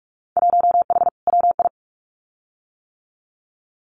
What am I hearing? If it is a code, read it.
Morse code, heard as a keyed tone, about 46 wpm, 700 Hz. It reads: 1H WI
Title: Winter Field Day Exchanges 45wpm